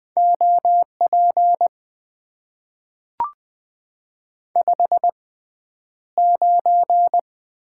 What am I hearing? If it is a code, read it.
Morse code, heard as a keyed tone, about 20 wpm, spaced as for 10 wpm, 700 Hz. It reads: OP 59
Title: Top 90 QSO Elements - 6x Character Spacing - 20wpm